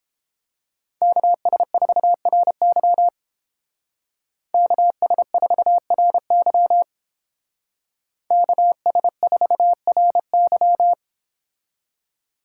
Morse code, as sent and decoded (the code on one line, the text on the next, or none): -.- ... ....- .-. -.-- / -.- ... ....- .-. -.-- / -.- ... ....- .-. -.--
KS4RY KS4RY KS4RY